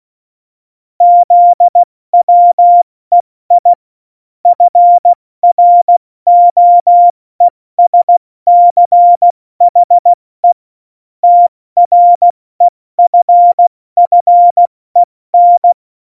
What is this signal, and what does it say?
Morse code, heard as a keyed tone, about 16 wpm, spaced as for 14 wpm, 700 Hz. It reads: ZWEI FROESCHE TREFFEN
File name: Morsekurs / Witze / Witz 20 March 2018, 70 F80